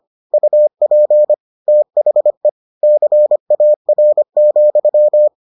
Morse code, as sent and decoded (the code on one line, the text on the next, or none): ..- .--. / - .... / -.-. .- .-. --..--
UP THE CAR,